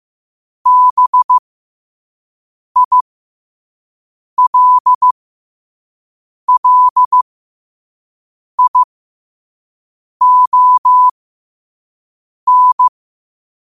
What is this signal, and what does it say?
Morse code, heard as a keyed tone, about 15 wpm, spaced as for 5 wpm, 1000 Hz. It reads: BILLION
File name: qotdfiles / 5wpm